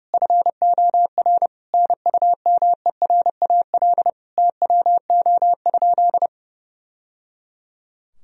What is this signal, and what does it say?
Morse code, heard as a keyed tone, about 30 wpm, 700 Hz. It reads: FOR NUMERAL TWO?
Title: Sentences from Top 1000 Words - Review Part 2 - 30wpm